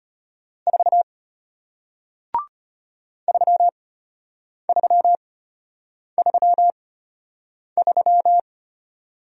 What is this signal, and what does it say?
Morse code, heard as a keyed tone, about 33 wpm, spaced as for 8 wpm, 700 Hz. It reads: 4 3333